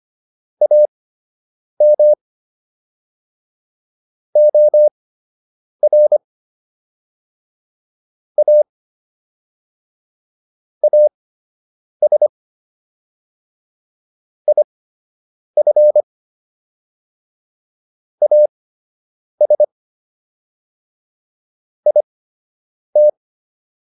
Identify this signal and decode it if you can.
Morse code, heard as a keyed tone, about 25 wpm, spaced as for 8 wpm, 600 Hz. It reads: AM OR A AS IF AS IT